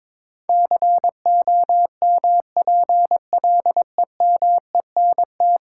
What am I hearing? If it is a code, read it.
Morse code, heard as a keyed tone, about 22 wpm, 700 Hz. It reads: COMPLEMENT
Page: morse-code-ninja-practice-22wpm